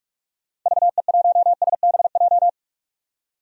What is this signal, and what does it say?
Morse code, heard as a keyed tone, about 44 wpm, 700 Hz. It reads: VE1SBJ
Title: DXCC-100 Call Signs - 6 Characters - 45wpm